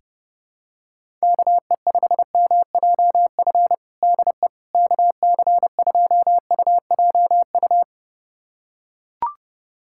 Morse code, heard as a keyed tone, about 30 wpm, 700 Hz. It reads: KE5MJF DE KC2UJU
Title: USA Callsign DE Callsign - 6 Characters - 30wpm